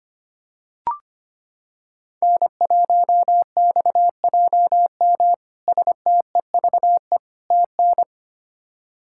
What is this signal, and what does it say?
Morse code, heard as a keyed tone, about 25 wpm, 700 Hz. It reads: N1XJM STEVE TN
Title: K1USN SST QSO Practice - 25wpm